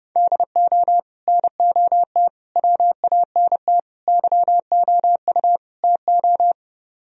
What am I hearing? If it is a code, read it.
Morse code, heard as a keyed tone, about 37 wpm, 700 Hz. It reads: DO NOT WANT YOU TO
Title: Speed-Racing - Sets of 5 Words 30wpm